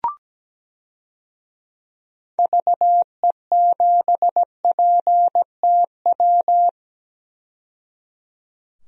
Morse code, heard as a keyed tone, about 17 wpm, 700 Hz. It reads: VE7PTW
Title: DXCC-100 Call Signs - 6 Characters - 17wpm